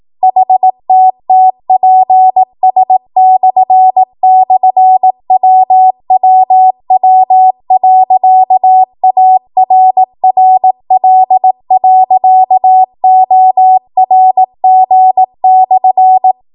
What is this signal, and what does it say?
Morse code, heard as a keyed tone, about 18 wpm, 750 Hz. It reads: HTTPS//WWW.ARRL.ORG/
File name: Morse / 18WPM